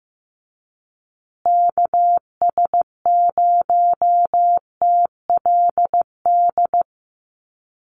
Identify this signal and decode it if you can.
Morse code, heard as a keyed tone, about 15 wpm, 700 Hz. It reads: KS0TLD
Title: USA Call Signs Repeated 3x 15wpm